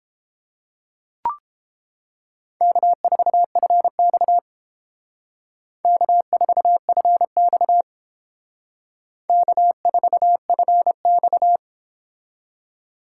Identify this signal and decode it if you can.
Morse code, heard as a keyed tone, about 28 wpm, 700 Hz. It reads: K4FX K4FX K4FX